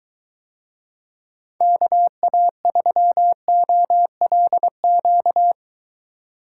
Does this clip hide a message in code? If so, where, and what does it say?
Yes: Morse code from 1.6 to 5.5 s, heard as a keyed tone, about 22 wpm, 700 Hz. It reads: KA3OLQ